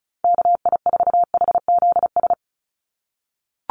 Morse code, heard as a keyed tone, about 34 wpm, 700 Hz. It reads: KI4HZS
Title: USA Call Signs 35wpm